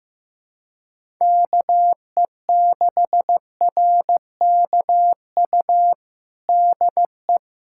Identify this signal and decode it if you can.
Morse code, heard as a keyed tone, about 15 wpm, 700 Hz. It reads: KE6RKU DE
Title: USA Callsign DE Callsign 15wpm